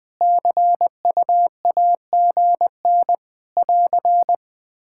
Morse code, heard as a keyed tone, about 20 wpm, 700 Hz. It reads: CUAGN <AR>
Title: Practice QSO - Exchange 4 - Ending - 20wpm